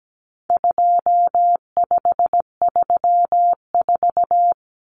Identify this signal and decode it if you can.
Morse code, heard as a keyed tone, about 17 wpm, 700 Hz. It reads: 2534
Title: Numbers - 4-Digit 17wpm